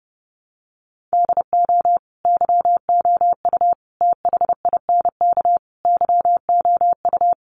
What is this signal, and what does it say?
Morse code, heard as a keyed tone, about 30 wpm, 700 Hz. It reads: DO YOU THINK YOU
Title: Sets of 4 Words 30wpm